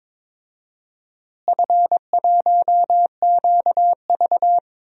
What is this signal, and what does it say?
Morse code, heard as a keyed tone, about 22 wpm, 700 Hz. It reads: F1QV
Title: DXCC-100 Call Signs - 4 Characters - Repeated 2x - 22wpm